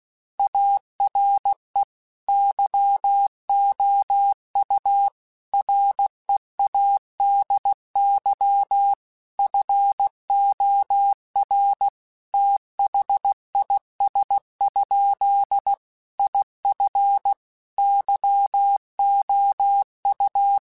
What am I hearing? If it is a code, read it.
Morse code, heard as a keyed tone, about 16 wpm, 800 Hz. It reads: ARE YOU READY FOR THIS? IF YOU
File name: Technical Subjects / Technical Subjects-mp3s / OR Gate @15